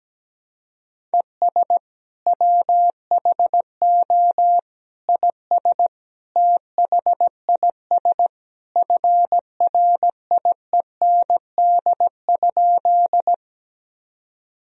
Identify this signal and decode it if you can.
Morse code, heard as a keyed tone, about 17 wpm, 700 Hz. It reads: ES WHO IS THIS FRIEND?
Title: Sentences from Top 300 Words and Abbreviations 17wpm